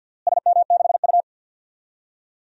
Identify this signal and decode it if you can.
Morse code, heard as a keyed tone, about 47 wpm, 700 Hz. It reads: SM6U
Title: DXCC-100 Call Signs - 4 Characters - Repeated 2x - 50wpm